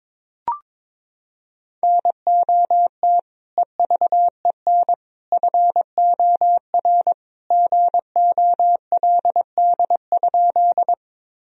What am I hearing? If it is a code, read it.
Morse code, heard as a keyed tone, about 22 wpm, 700 Hz. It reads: NOT EVEN FOR GOLD?
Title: Sentences from Top 1000 Words - Review Part 3 - 22wpm